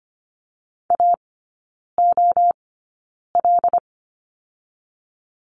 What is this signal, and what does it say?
Morse code, heard as a keyed tone, about 25 wpm, spaced as for 9 wpm, 700 Hz. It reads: AOL
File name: course / 25:10wpm / 093 Lesson 16e - Letter c - Three Characters 25wpm